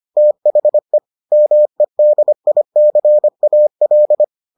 Morse code, heard as a keyed tone, about 25 wpm, 600 Hz. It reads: THE MEDICAL